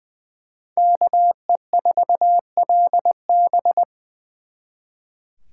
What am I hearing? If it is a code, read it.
Morse code, heard as a keyed tone, about 24 wpm, 700 Hz. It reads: KE4LB
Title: Speed Racing - USA Call Signs 20wpm